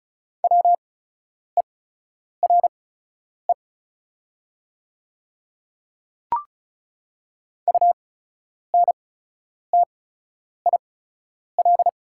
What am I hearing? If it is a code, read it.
Morse code, heard as a keyed tone, about 35 wpm, spaced as for 10 wpm, 690 Hz. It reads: WERE UNTIL
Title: 8x Character Spacing - Top 300 Words - 35wpm